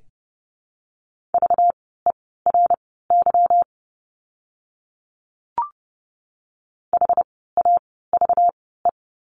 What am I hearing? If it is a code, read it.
Morse code, heard as a keyed tone, about 29 wpm, spaced as for 17 wpm, 700 Hz. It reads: VERY HAVE
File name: course / 30:10wpm / 230 Lesson 37f - Letter x - Words 30wpm